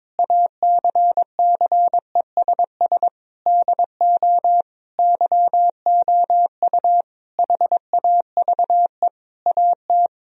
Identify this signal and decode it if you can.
Morse code, heard as a keyed tone, about 22 wpm, 700 Hz. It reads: ACCESS DO YOU HAVE AT
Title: Sentences from Top 1000 Words - Review Part 4 - 22wpm